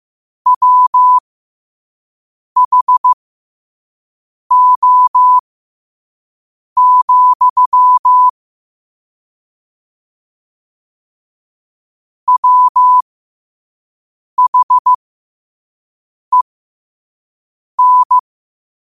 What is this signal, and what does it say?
Morse code, heard as a keyed tone, about 15 wpm, spaced as for 5 wpm, 1000 Hz. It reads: WHO, WHEN
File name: qotdfiles / 5wpm